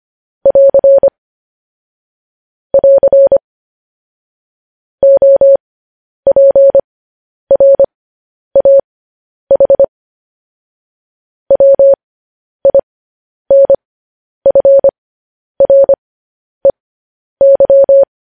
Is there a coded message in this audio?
Yes: Morse code from 0.5 to 18.0 s, heard as a keyed tone, about 25 wpm, spaced as for 10 wpm, 550 Hz. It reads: <AR> <AR> OPRAH WINFREY